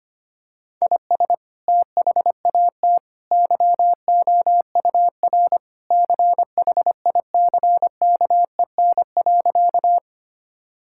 Morse code, heard as a keyed tone, about 25 wpm, 700 Hz. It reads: IS THAT YOUR CHICKEN.
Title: Sentences from Top 1000 Words - Review Part 5 - 25wpm